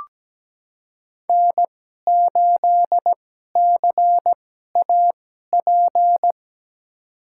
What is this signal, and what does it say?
Morse code, heard as a keyed tone, about 17 wpm, spaced as for 12 wpm, 700 Hz. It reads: N8CAP
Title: USA Call Signs - 2x Character Spacing - 17wpm